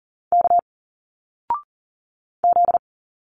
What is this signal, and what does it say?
Morse code, heard as a keyed tone, about 40 wpm, 700 Hz. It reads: K Z
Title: Single Letters 40wpm